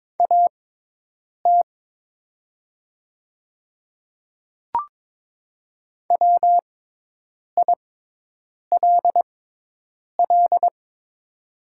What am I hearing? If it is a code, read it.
Morse code, heard as a keyed tone, about 22 wpm, spaced as for 8 wpm, 700 Hz. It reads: AT WILL
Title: 6x Character Spacing - Top 200 Words - 22wpm